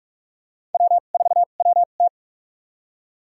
Morse code, heard as a keyed tone, about 41 wpm, spaced as for 31 wpm, 690 Hz. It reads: W4WT
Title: USA Call Signs - 4 Characters - 2x Character Spacing - 45wpm